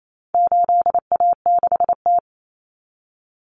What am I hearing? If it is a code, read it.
Morse code, heard as a keyed tone, about 28 wpm, 700 Hz. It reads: 8A6T
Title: Common Cut Numbers - 4-Digit 28wpm